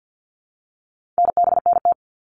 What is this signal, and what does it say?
Morse code, heard as a keyed tone, about 50 wpm, 700 Hz. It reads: NBNT
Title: All Cut Numbers - 4-Digit 50wpm